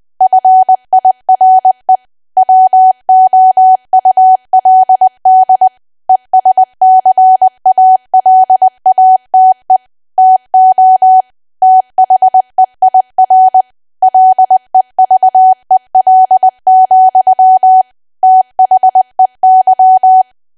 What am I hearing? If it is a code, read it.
Morse code, heard as a keyed tone, about 20 wpm, 750 Hz. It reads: FIRE WOULD ESCALATE TO THEIR LEVEL, THEY
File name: Archive / 20 WPM / 20WPM